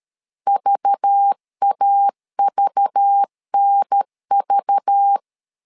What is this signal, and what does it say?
Morse code, heard as a keyed tone, about 13 wpm, 800 Hz. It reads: VAVNV
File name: LEZIONI / 5 LBVC / 5 LBV 12wpm 1855